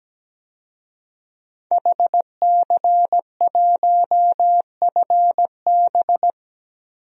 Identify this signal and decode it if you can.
Morse code, heard as a keyed tone, about 17 wpm, 700 Hz. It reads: HC1FB
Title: DXCC-100 Call Signs - 5 Characters - 17wpm